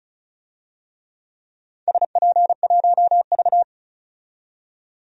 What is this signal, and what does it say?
Morse code, heard as a keyed tone, about 36 wpm, 700 Hz. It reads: SP1V
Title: DXCC-100 Call Signs - 4 Characters - 35wpm